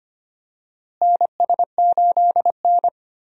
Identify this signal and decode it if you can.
Morse code, heard as a keyed tone, about 25 wpm, 700 Hz. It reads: NS8N
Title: USA Call Signs - 4 Characters - Repeated 3x - 25wpm